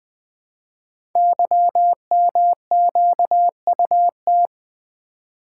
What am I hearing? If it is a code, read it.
Morse code, heard as a keyed tone, about 20 wpm, 700 Hz. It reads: YMQUT
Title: Code Groups - 5 Letters - 20wpm